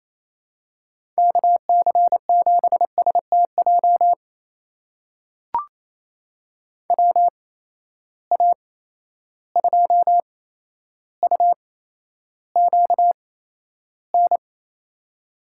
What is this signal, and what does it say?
Morse code, heard as a keyed tone, about 28 wpm, spaced as for 12 wpm, 700 Hz. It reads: KC7STJ WA2UQN
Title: USA Call Signs - 6 Characters - 8x Character Spacing - 28wpm